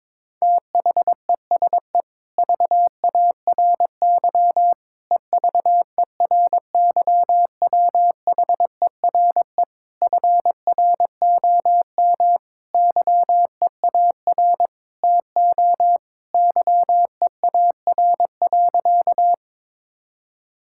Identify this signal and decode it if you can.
Morse code, heard as a keyed tone, about 22 wpm, 700 Hz. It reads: THESE VARY EVERYWHERE FROM YEAR TO YEAR.
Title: Sentences from Top 1600 Words 22wpm